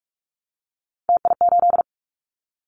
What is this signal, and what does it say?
Morse code, heard as a keyed tone, about 46 wpm, 700 Hz. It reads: TI8
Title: Callsign Prefixes for Top 100 DXCC - 45wpm